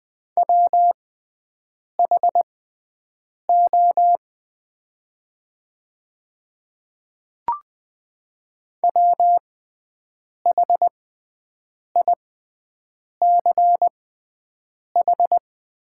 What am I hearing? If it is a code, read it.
Morse code, heard as a keyed tone, about 20 wpm, spaced as for 7 wpm, 700 Hz. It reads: WHO WHICH
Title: 6x Character Spacing - Top 100 Words - 20wpm